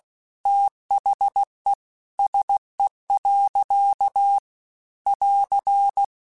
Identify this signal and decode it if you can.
Morse code, heard as a keyed tone, about 16 wpm, 800 Hz. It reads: THE SE. <AR>
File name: Weather Sets / Weather Sets-mp3s / Weather 4 17 @15